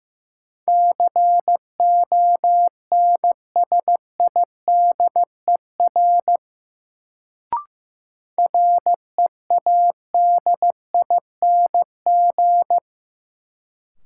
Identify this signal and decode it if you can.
Morse code, heard as a keyed tone, about 15 wpm, 700 Hz. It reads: CONSIDER READING
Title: Sentences from Top 600 Words - New Word Drill - 15wpm